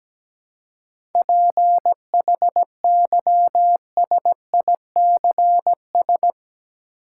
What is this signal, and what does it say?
Morse code, heard as a keyed tone, about 17 wpm, 700 Hz. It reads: PHYSICS
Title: Seven Letter Words - 17wpm